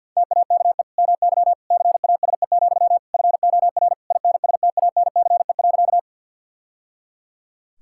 Morse code, heard as a keyed tone, about 46 wpm, 700 Hz. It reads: TAKE MY CASE, FOR INSTANCE.
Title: Sentences from Top 1000 Words - Review Part 5 - 50wpm